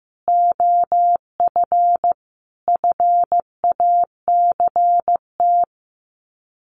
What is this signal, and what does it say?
Morse code, heard as a keyed tone, about 15 wpm, 700 Hz. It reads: OF FACT
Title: Sets of 4 Words 15wpm